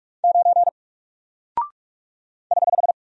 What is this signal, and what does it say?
Morse code, heard as a keyed tone, about 44 wpm, 700 Hz. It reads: 9 <HH>
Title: Single Character (Letter:Number:Punctuation:Prosign) 45wpm